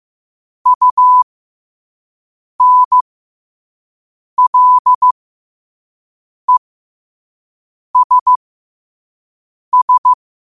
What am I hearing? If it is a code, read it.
Morse code, heard as a keyed tone, about 15 wpm, spaced as for 5 wpm, 1000 Hz. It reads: UNLESS
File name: qotdfiles / 5wpm